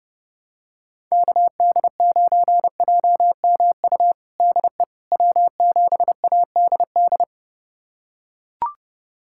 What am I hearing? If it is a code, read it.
Morse code, heard as a keyed tone, about 30 wpm, 700 Hz. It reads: KD9JMU DE W7ADD
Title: USA Callsign DE Callsign 30wpm